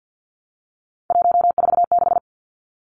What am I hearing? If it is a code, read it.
Morse code, heard as a keyed tone, about 50 wpm, 700 Hz. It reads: 146
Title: Numbers - 3-Digit 50wpm